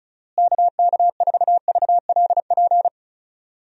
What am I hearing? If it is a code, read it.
Morse code, heard as a keyed tone, about 34 wpm, 700 Hz. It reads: KK4VLP
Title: USA Call Signs - 6 Characters - Repeated 3x - 35wpm